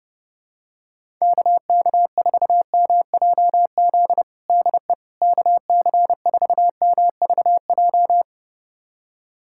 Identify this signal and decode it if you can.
Morse code, heard as a keyed tone, about 30 wpm, 700 Hz. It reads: KK4MJZ DE KC4MVJ
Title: USA Callsign DE Callsign - 6 Characters - 30wpm